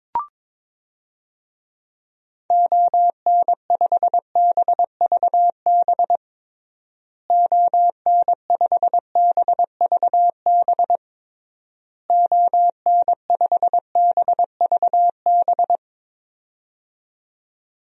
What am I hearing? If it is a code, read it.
Morse code, heard as a keyed tone, about 22 wpm, 700 Hz. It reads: ON5BVB ON5BVB ON5BVB
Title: DXCC-100 Call Signs - 6 Characters - Repeated 3x - 22wpm